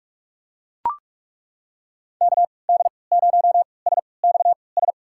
Morse code, heard as a keyed tone, about 42 wpm, spaced as for 25 wpm, 700 Hz. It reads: KD0SXS
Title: USA Call Signs - 6 Characters - 3x Character Spacing - 45wpm